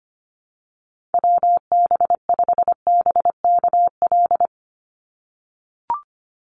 Morse code, heard as a keyed tone, about 25 wpm, 700 Hz. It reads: WB5BKL
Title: NA SOTA Chaser Callsigns 25wpm